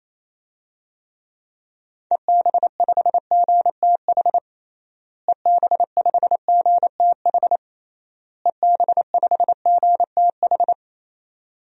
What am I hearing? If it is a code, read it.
Morse code, heard as a keyed tone, about 28 wpm, 700 Hz. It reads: EB5GTH EB5GTH EB5GTH
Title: DXCC-100 Call Signs - 6 Characters - Repeated 3x - 28wpm